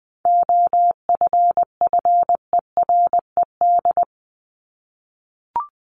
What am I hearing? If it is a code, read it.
Morse code, heard as a keyed tone, about 20 wpm, 700 Hz. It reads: OFFERED
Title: ED Suffix - Top 100 Words with Base Words 20wpm